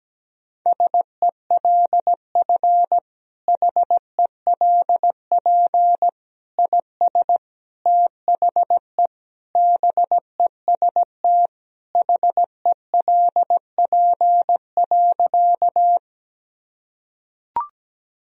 Morse code, heard as a keyed tone, about 17 wpm, 700 Hz. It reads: SELF HELP IS THE BEST HELP.
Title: Sentences from Top 1000 Words - Review Part 1 - 17wpm